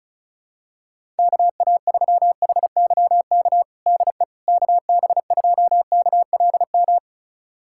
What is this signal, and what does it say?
Morse code, heard as a keyed tone, about 34 wpm, 700 Hz. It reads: KA3HYK DE KB2KLM
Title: USA Callsign DE Callsign - 6 Characters - 35wpm